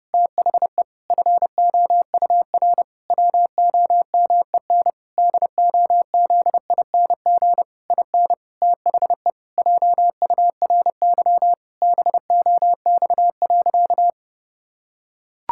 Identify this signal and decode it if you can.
Morse code, heard as a keyed tone, about 32 wpm, 700 Hz. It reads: THE FOUR WOMEN DOZING IN THE JURY BOX.
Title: Pangram Challenge v2 - 30wpm